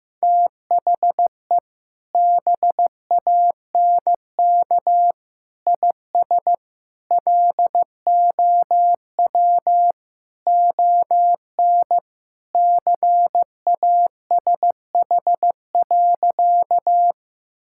Morse code, heard as a keyed tone, about 15 wpm, 700 Hz. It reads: THE BANK IS LOW ON CASH.
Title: Sentences from Top 1000 Words - Review Part 5 - 15wpm